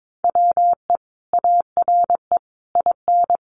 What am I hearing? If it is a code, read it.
Morse code, heard as a keyed tone, about 22 wpm, 700 Hz. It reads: WE ARE IN
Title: Sentences from Top 1400 Words 22wpm